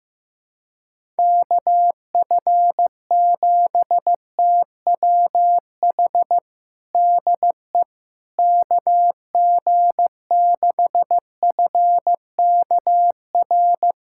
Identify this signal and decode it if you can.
Morse code, heard as a keyed tone, about 15 wpm, 700 Hz. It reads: KF7TWH DE KG6FKR
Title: USA Callsign DE Callsign - 6 Characters - 15wpm